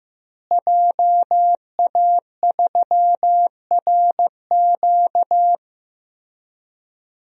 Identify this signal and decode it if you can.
Morse code, heard as a keyed tone, about 15 wpm, 700 Hz. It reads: JA3RQ
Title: DXCC-100 Call Signs - 5 Characters - Repeated 3x - 15wpm